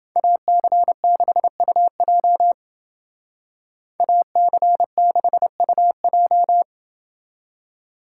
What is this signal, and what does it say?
Morse code, heard as a keyed tone, about 28 wpm, 700 Hz. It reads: AC6UJ AC6UJ